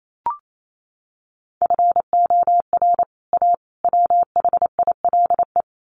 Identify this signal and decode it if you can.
Morse code, heard as a keyed tone, about 28 wpm, 700 Hz. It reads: FOR A WHILE
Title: Sets of 3 Words 28wpm